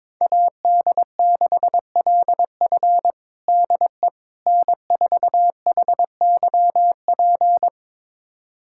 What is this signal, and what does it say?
Morse code, heard as a keyed tone, about 22 wpm, 700 Hz. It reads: AD6LF DE N4HYP
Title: USA Callsign DE Callsign - 5 Characters - 22wpm